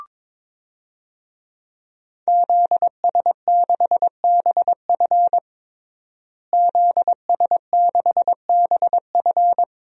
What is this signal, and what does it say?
Morse code, heard as a keyed tone, about 22 wpm, 700 Hz. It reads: ZS6BF ZS6BF
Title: DXCC-100 Call Signs - 5 Characters - Repeated 2x - 22wpm